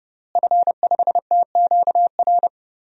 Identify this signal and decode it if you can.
Morse code, heard as a keyed tone, about 30 wpm, 700 Hz. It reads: F5TQR
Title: DXCC-100 Call Signs - 5 Characters - Repeated 3x - 30wpm